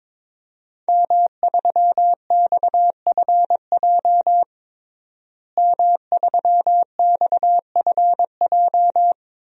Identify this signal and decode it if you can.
Morse code, heard as a keyed tone, about 21 wpm, 700 Hz. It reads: M3XFJ M3XFJ
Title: DXCC-100 Call Signs - 5 Characters - Repeated 2x - 22wpm